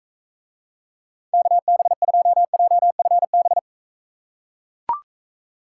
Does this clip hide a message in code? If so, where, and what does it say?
Yes: Morse code from 1.3 to 3.6 s, heard as a keyed tone, about 39 wpm, 700 Hz. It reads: KB2JFB